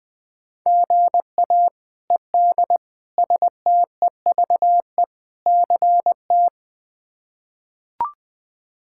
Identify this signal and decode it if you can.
Morse code, heard as a keyed tone, about 21 wpm, 700 Hz. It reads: GA ED STEVE CT
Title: K1USN SST QSO Practice - 20wpm